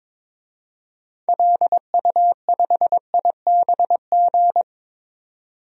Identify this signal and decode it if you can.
Morse code, heard as a keyed tone, about 22 wpm, 700 Hz. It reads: LU5IBG
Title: DXCC-100 Call Signs - 6 Characters - 22wpm